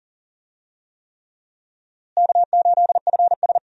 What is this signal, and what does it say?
Morse code, heard as a keyed tone, about 40 wpm, 700 Hz. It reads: K8FS
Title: USA Call Signs - 4 Characters - Repeated 3x - 40wpm